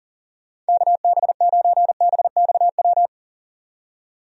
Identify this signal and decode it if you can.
Morse code, heard as a keyed tone, about 40 wpm, 700 Hz. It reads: KB9BXW
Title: USA Call Signs - 6 Characters - 40wpm